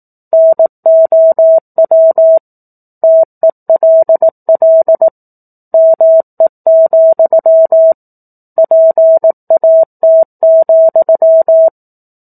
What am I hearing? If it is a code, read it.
Morse code, heard as a keyed tone, about 18 wpm, 650 Hz. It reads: NOW TELL ME, PAT,
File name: 04-Chapter-04 18 WPM